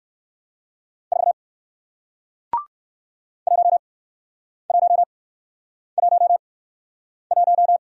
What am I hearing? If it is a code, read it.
Morse code, heard as a keyed tone, about 59 wpm, spaced as for 9 wpm, 700 Hz. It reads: 4 1111